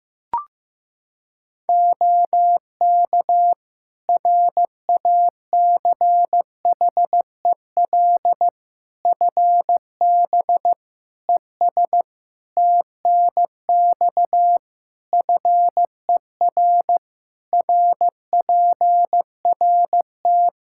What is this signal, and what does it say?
Morse code, heard as a keyed tone, about 15 wpm, 700 Hz. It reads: OK RACHEL FB ES TNX FER RPRT
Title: Practice QSO - Exchange 2 - Info - 15wpm